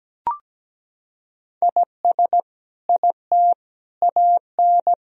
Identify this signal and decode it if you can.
Morse code, heard as a keyed tone, about 17 wpm, 700 Hz. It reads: IS IT AN
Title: Sentences from Top 1000 Words - Review Part 3 - 17wpm